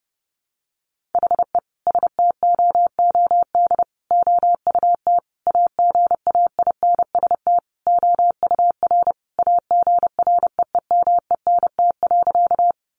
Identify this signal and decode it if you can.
Morse code, heard as a keyed tone, about 30 wpm, 700 Hz. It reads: HE STOOD OUT AGAINST OUR AGREEMENT.
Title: Sentences from Top 1500 Words 30wpm